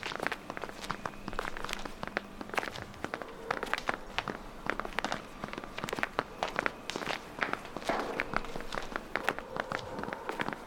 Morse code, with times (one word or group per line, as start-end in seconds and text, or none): none